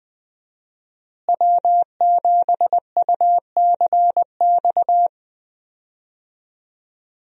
Morse code, heard as a keyed tone, about 20 wpm, 700 Hz. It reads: W7UCX
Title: USA Call Signs - 5 Characters - Repeated 3x - 20wpm